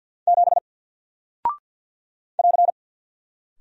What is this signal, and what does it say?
Morse code, heard as a keyed tone, about 48 wpm, 700 Hz. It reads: 7 <AR>
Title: Single Character (Letter:Number:Punctuation:Prosign) Alt 50wpm